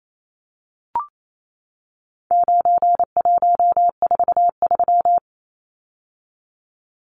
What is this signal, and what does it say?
Morse code, heard as a keyed tone, about 28 wpm, 700 Hz. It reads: 9143
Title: Numbers - 4-Digit 28wpm